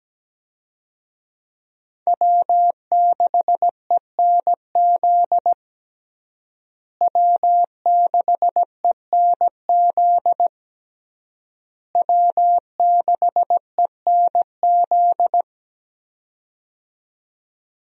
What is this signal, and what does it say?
Morse code, heard as a keyed tone, about 17 wpm, 700 Hz. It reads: W6ENZ W6ENZ W6ENZ
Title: USA Call Signs - 5 Characters - Repeated 3x - 17wpm